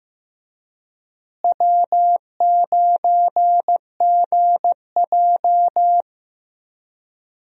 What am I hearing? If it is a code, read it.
Morse code, heard as a keyed tone, about 15 wpm, 700 Hz. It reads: W9GJ
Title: USA Call Signs - 4 Characters - Repeated 3x - 15wpm